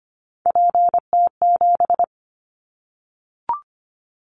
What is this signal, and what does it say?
Morse code, heard as a keyed tone, about 25 wpm, 700 Hz. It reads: PT7
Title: Callsign Prefixes for Top 50 DXCC - 25wpm